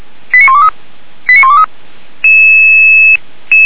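Five-tone selective call (ZVEI), heard as quick runs of five tones, tones of about 80 ms. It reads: 89123 89123
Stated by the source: ZVEI five-tone selective call received on SDR.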